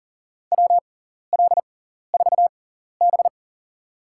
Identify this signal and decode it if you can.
Morse code, heard as a keyed tone, about 37 wpm, spaced as for 14 wpm, 700 Hz. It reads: WL4B
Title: USA Call Signs - 4 Characters - 6x Character Spacing - 40wpm